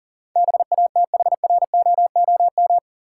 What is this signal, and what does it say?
Morse code, heard as a keyed tone, about 40 wpm, 700 Hz. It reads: BATHROOM
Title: Sets of 5 Words Encore - New Word Drill - 40wpm